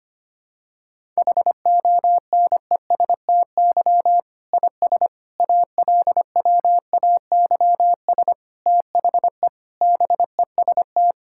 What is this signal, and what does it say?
Morse code, heard as a keyed tone, about 25 wpm, 700 Hz. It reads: HONESTY IS ALWAYS THE BEST
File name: practice / 25wpm / Difficult Sentences with Hints - 25wpm